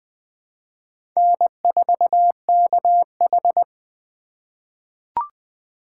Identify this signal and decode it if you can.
Morse code, heard as a keyed tone, about 20 wpm, 700 Hz. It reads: N4KH
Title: K1USN SST QSO Practice - 20wpm